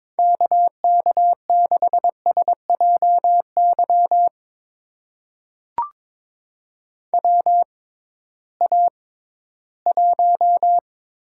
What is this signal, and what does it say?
Morse code, heard as a keyed tone, about 22 wpm, spaced as for 14 wpm, 700 Hz. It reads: KK6SJY WA1
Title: USA Call Signs - 6 Characters - 6x Character Spacing - 22wpm